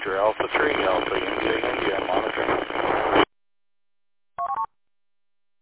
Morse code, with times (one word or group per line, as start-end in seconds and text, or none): none